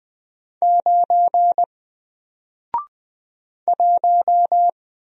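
Morse code, heard as a keyed tone, about 20 wpm, 700 Hz. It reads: 9 1
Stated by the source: Single letter and number Morse drill at 20 wpm, from N4LCD.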